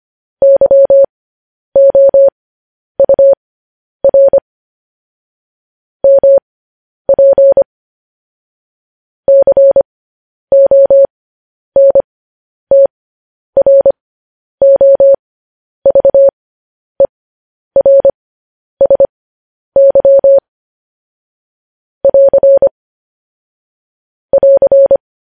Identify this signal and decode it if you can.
Morse code, heard as a keyed tone, about 25 wpm, spaced as for 10 wpm, 550 Hz. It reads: YOUR MP CONTROVERSY <AR> <AR>
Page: News Headlines in Morse Code at 10 WPM